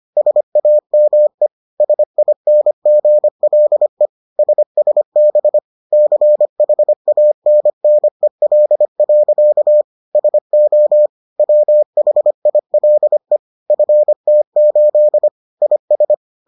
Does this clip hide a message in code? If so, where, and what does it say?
Yes: Morse code from 0.2 to 16.1 s, heard as a keyed tone, about 25 wpm, 600 Hz. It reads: SAME SINGLE SSB CHANNEL. SO WHILE FT8 IS